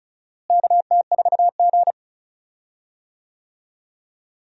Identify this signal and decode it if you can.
Morse code, heard as a keyed tone, about 36 wpm, 700 Hz. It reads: KT4G
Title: USA Call Signs - 4 Characters - Repeated 3x - 35wpm